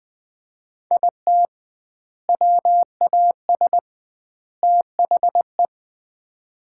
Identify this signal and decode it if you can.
Morse code, heard as a keyed tone, about 20 wpm, 700 Hz. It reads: IT WAS THE
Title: 2x Word-Spacing - Sets of 3 Words - 20wpm